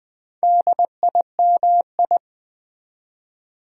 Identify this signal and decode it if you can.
Morse code, heard as a keyed tone, about 20 wpm, 700 Hz. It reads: DIMI
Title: Build A Word - 4 Letters At A Time - 8-Letter Words - 20wpm